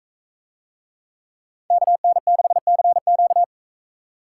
Morse code, heard as a keyed tone, about 44 wpm, 700 Hz. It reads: KN6CQ